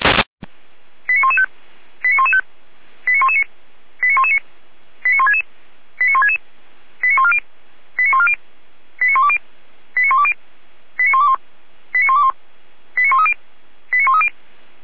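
Five-tone selective call (ZVEI), heard as quick runs of five tones, tones of about 70 ms. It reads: 89115 89115 89119 89119 89166 89166 89140 89140 89120 89120 89121 89121 89130 89130
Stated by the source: ZVEI five-tone selective call received on an SDR.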